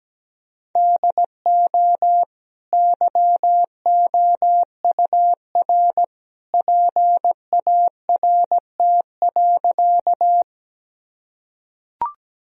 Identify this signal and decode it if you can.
Morse code, heard as a keyed tone, about 17 wpm, 700 Hz. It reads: DO YOUR PART.
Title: Sentences from Top 1000 Words - Review Part 1 - 17wpm